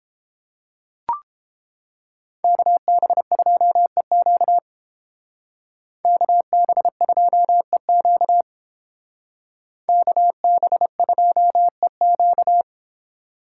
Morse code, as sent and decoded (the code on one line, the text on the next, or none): -.- -... ..--- . --.- / -.- -... ..--- . --.- / -.- -... ..--- . --.-
KB2EQ KB2EQ KB2EQ